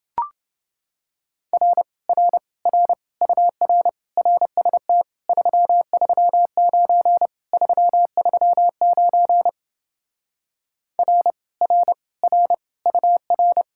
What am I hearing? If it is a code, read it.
Morse code, heard as a keyed tone, about 30 wpm, 700 Hz. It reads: R R R UR RST 339 339 R R R UR